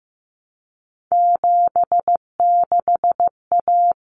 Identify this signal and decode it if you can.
Morse code, heard as a keyed tone, about 15 wpm, 700 Hz. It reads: 76A
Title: Common Cut Numbers - 3-Digit 15wpm